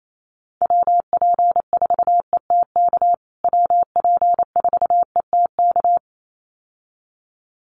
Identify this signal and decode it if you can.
Morse code, heard as a keyed tone, about 28 wpm, 700 Hz. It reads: WP4ETK WP4ETK
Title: USA Call Signs Repeated 2x 28wpm